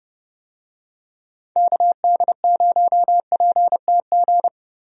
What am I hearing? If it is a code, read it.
Morse code, heard as a keyed tone, about 30 wpm, 700 Hz. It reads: KD0PTG
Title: USA Call Signs - 6 Characters - Repeated 3x - 30wpm